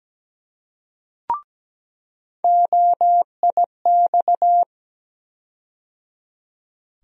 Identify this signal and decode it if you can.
Morse code, heard as a keyed tone, about 17 wpm, 700 Hz. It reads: OIX
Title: USA Call Sign Postfixes - 3 Letters 17wpm